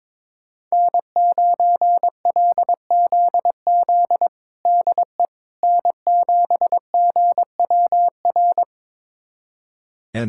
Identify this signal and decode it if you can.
Morse code, heard as a keyed tone, about 21 wpm, 700 Hz. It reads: N9LZZ DE N7GWR
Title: USA Callsign DE Callsign - 5 Characters - 22wpm